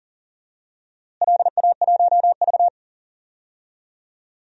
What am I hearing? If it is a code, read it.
Morse code, heard as a keyed tone, about 40 wpm, 700 Hz. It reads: LA1V
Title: DXCC-100 Call Signs - 4 Characters - Repeated 3x - 40wpm